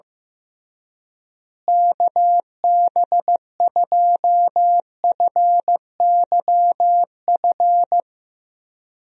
Morse code, heard as a keyed tone, about 16 wpm, 700 Hz. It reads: KB2FYF
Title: USA Call Signs - 6 Characters - Repeated 3x - 15wpm